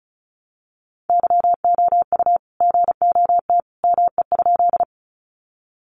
Morse code, heard as a keyed tone, about 34 wpm, 700 Hz. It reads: YOU GOT ME?
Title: Sentences from Top 300 Words 35wpm